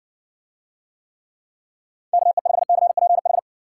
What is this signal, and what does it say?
Morse code, heard as a keyed tone, about 58 wpm, 700 Hz. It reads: KE5CPH